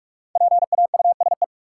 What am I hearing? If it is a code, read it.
Morse code, heard as a keyed tone, about 45 wpm, 700 Hz. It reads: PAUSE
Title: Five Letter Words - 45wpm